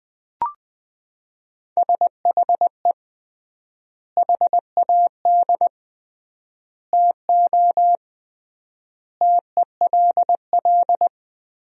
Morse code, heard as a keyed tone, about 20 wpm, 700 Hz. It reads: SHE HAD TO TELL
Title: 3x Word-Spacing - Sentences from Top 200 Words - 20wpm